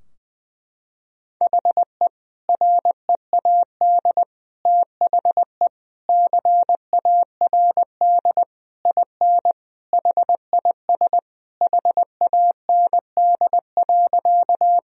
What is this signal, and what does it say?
Morse code, heard as a keyed tone, about 20 wpm, 700 Hz. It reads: HE READ THE CARD IN HIS HAND.
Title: Sentences from Top 1000 Words - Review Part 4 - 20wpm